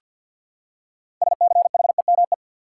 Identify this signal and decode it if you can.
Morse code, heard as a keyed tone, about 47 wpm, 700 Hz. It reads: SCHEME
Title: Six Letter Words - 50wpm